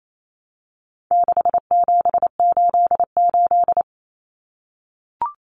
Morse code, heard as a keyed tone, about 28 wpm, 700 Hz. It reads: 6788
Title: Numbers - 4-Digit 28wpm